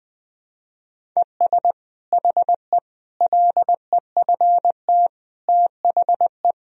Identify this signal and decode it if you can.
Morse code, heard as a keyed tone, about 21 wpm, 700 Hz. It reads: ES HE LEFT THE
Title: Sentences from Top 300 Words and Abbreviations 20wpm